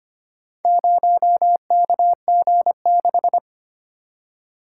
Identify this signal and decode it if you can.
Morse code, heard as a keyed tone, about 25 wpm, 700 Hz. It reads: 0KG6
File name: practice / 25wpm / Four Letter-Number Combinations - 25wpm